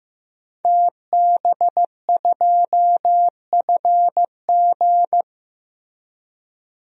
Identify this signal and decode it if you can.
Morse code, heard as a keyed tone, about 15 wpm, 700 Hz. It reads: TB2FG
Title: DXCC-100 Call Signs - 5 Characters - Repeated 3x - 15wpm